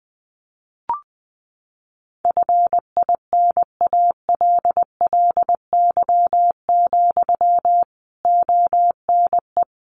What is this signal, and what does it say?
Morse code, heard as a keyed tone, about 20 wpm, 700 Hz. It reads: FINALLY, ONE
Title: Sentences from Top 1300 Words 20wpm